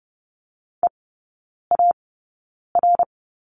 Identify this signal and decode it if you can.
Morse code, heard as a keyed tone, about 30 wpm, spaced as for 9 wpm, 700 Hz. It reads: EAR
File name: course / 30:10wpm / 087 Lesson 15e - Number 5 - Three Characters 30wpm